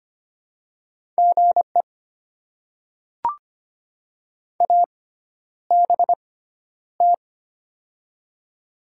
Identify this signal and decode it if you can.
Morse code, heard as a keyed tone, about 25 wpm, spaced as for 11 wpm, 700 Hz. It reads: GE ABT
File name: Top 90 QSO Elements - 6x Character Spacing - 25wpm